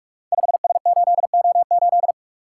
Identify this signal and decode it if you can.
Morse code, heard as a keyed tone, about 44 wpm, 700 Hz. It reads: 5S8O8
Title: Code Groups - 5 Letter:Numbers - 45wpm